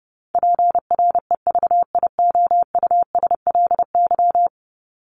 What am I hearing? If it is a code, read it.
Morse code, heard as a keyed tone, about 30 wpm, 700 Hz. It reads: PREVIOUSLY
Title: LY Suffix - Top 100 Words with Base Words 30wpm